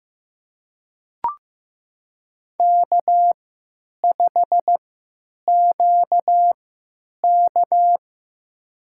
Morse code, heard as a keyed tone, about 15 wpm, spaced as for 9 wpm, 700 Hz. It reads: K5QK
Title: USA Call Signs - 4 Characters - 3x Character Spacing - 15wpm